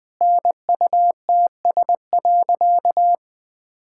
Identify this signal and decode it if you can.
Morse code, heard as a keyed tone, about 21 wpm, 700 Hz. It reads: NUTS.
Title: Easy Sentences with Hints - 20wpm